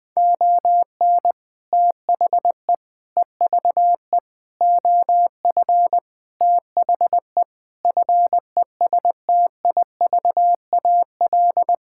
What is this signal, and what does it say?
Morse code, heard as a keyed tone, about 20 wpm, 700 Hz. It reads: ON THE EVE OF THE FESTIVAL
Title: Sentences with 5-Word Phrases - 20wpm